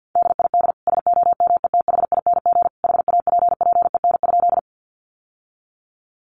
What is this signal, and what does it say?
Morse code, heard as a keyed tone, about 50 wpm, 700 Hz. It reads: DID SOMETHING HAPPEN?
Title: Sentences from Top 600 Words 50wpm